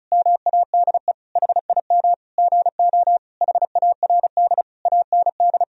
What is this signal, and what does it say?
Morse code, heard as a keyed tone, about 36 wpm, 700 Hz. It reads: MADE HIM GO HARD AND
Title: Sentences from Top 1000 Words - Review Part 1 - 35wpm